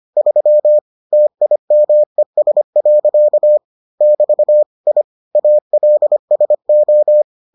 Morse code, heard as a keyed tone, about 25 wpm, 600 Hz. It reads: 3 TIMES. <BT> I ALSO